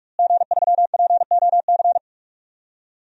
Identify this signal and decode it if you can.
Morse code, heard as a keyed tone, about 41 wpm, 700 Hz. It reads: G3POC
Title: DXCC-100 Call Signs - 5 Characters - Repeated 3x - 45wpm